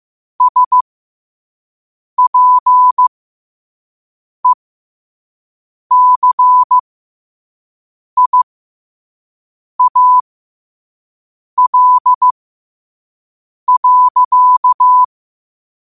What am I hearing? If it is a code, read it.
Morse code, heard as a keyed tone, about 15 wpm, spaced as for 5 wpm, 1000 Hz. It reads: SPECIAL.
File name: qotdfiles / 5wpm